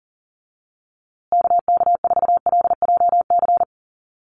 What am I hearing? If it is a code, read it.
Morse code, heard as a keyed tone, about 40 wpm, 700 Hz. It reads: KK4LJC
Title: USA Call Signs Repeated 3x 40wpm